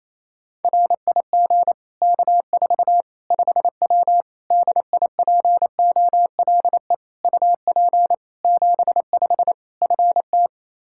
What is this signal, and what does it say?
Morse code, heard as a keyed tone, about 28 wpm, 700 Hz. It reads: RIG K4 5W DIPOLE UP 75 FT
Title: Practice QSO - Exchange 2 - Info - 28wpm